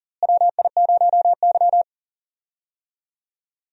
Morse code, heard as a keyed tone, about 38 wpm, 700 Hz. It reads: WI0Y
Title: USA Call Signs - 4 Characters - Repeated 2x - 40wpm